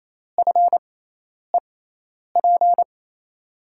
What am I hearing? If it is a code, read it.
Morse code, heard as a keyed tone, about 28 wpm, spaced as for 10 wpm, 700 Hz. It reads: FEP
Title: USA Call Signs - 6x Character Spacing - 28wpm